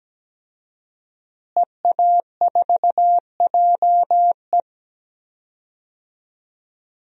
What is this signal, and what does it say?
Morse code, heard as a keyed tone, about 17 wpm, 700 Hz. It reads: EA4JE
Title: DXCC-100 Call Signs - 5 Characters - Repeated 2x - 17wpm